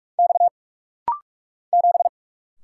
Morse code, heard as a keyed tone, about 46 wpm, 700 Hz. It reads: X 7
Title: Single Letter-Number Alt - Rapid-Fire 45wpm